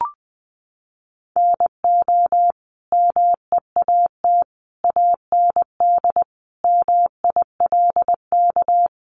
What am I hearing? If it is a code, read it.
Morse code, heard as a keyed tone, about 20 wpm, 700 Hz. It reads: NO MEAT AND MILK
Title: Sentences from Top 1500 Words 20wpm